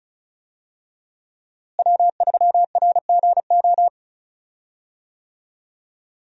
Morse code, heard as a keyed tone, about 34 wpm, 700 Hz. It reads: W3RGO
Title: USA Call Signs - 5 Characters - Repeated 3x - 35wpm